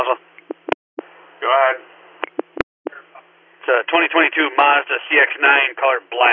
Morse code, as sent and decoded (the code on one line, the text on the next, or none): none